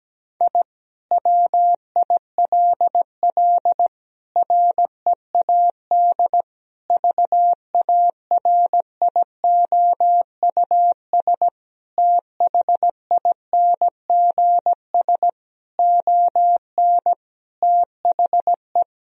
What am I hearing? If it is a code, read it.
Morse code, heard as a keyed tone, about 17 wpm, 700 Hz. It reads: I WILL READ VARIOUS THINGS ON THE
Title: Sentences from Top 1000 Words - Review Part 4 - 17wpm